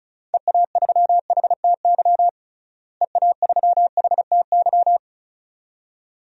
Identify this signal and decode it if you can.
Morse code, heard as a keyed tone, about 36 wpm, 700 Hz. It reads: EA3HTY EA3HTY
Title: DXCC-100 Call Signs - 6 Characters - Repeated 2x - 35wpm